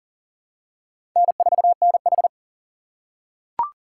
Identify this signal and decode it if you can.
Morse code, heard as a keyed tone, about 39 wpm, 700 Hz. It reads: N4NH
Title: USA Call Signs - 4 Characters - 2x Character Spacing - 40wpm